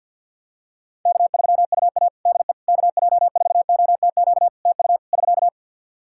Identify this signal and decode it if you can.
Morse code, heard as a keyed tone, about 48 wpm, 700 Hz. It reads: K3UA DE KJ4OTQ TU <SK>
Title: Practice QSO - Exchange 4 - Ending - 50wpm